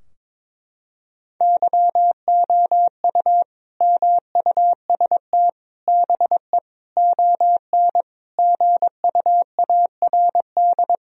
Morse code, heard as a keyed tone, about 22 wpm, 700 Hz. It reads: YOU MUST BE ON GUARD
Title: Sentences from Top 1000 Words - Review Part 4 - 22wpm